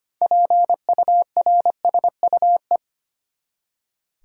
Morse code, heard as a keyed tone, about 25 wpm, 700 Hz. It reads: PURSUE
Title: Six Letter Words - 25wpm